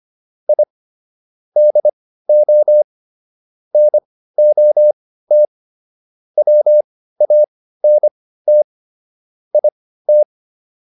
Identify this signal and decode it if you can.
Morse code, heard as a keyed tone, about 25 wpm, spaced as for 15 wpm, 600 Hz. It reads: I DO NOT WANT IT